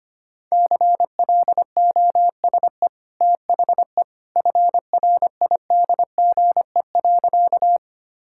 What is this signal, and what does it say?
Morse code, heard as a keyed tone, about 25 wpm, 690 Hz. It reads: CLOSE THE FRIDGE.
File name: Easy Sentences with Hints - 25wpm